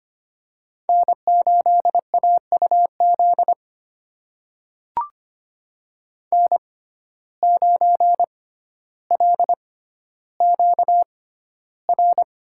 Morse code, heard as a keyed tone, about 25 wpm, spaced as for 13 wpm, 690 Hz. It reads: N8AUZ N9LQR
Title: USA Call Signs - 5 Characters - 6x Character Spacing - 25wpm